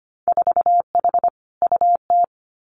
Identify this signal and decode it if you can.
Morse code, heard as a keyed tone, about 25 wpm, 690 Hz. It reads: 4H UT
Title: Winter Field Day Exchanges Encore 25wpm